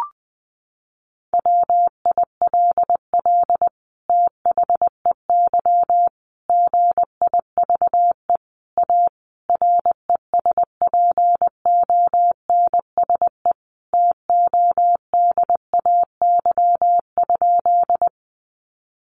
Morse code, heard as a keyed tone, about 20 wpm, 700 Hz. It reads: WILL THEY GIVE A RESPONSE TODAY?
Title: Sentences from Top 1400 Words 20wpm